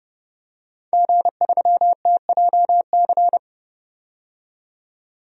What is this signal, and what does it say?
Morse code, heard as a keyed tone, about 32 wpm, 700 Hz. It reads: G3TJC
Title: DXCC-100 Call Signs - 5 Characters - 30wpm